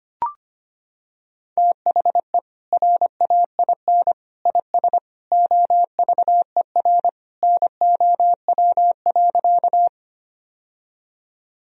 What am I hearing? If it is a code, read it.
Morse code, heard as a keyed tone, about 25 wpm, 700 Hz. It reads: THE RAIN IS OVER NOW.
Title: Sentences from Top 1000 Words - Review Part 1 - 25wpm